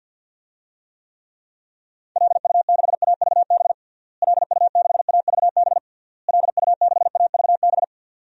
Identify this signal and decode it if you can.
Morse code, heard as a keyed tone, about 52 wpm, 700 Hz. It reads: LU6AVB LU6AVB LU6AVB
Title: DXCC-100 Call Signs - 6 Characters - Repeated 3x - 50wpm